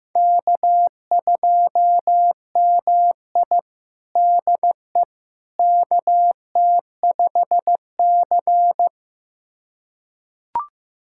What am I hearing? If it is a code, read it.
Morse code, heard as a keyed tone, about 16 wpm, 700 Hz. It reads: K2MI DE KT5C
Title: USA Callsign DE Callsign - 4 Characters - 15wpm